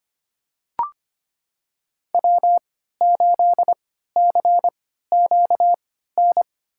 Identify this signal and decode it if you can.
Morse code, heard as a keyed tone, about 25 wpm, spaced as for 14 wpm, 700 Hz. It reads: W8CQN
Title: USA Call Signs - 5 Characters - 3x Character Spacing - 25wpm